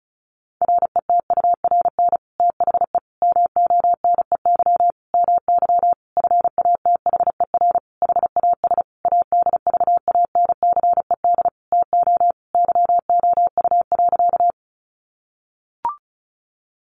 Morse code, heard as a keyed tone, about 34 wpm, 700 Hz. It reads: RETURN THE MONEY MY FATHER HAS ADVANCED TO YOU.
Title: Sentences from Top 1200 Words 35wpm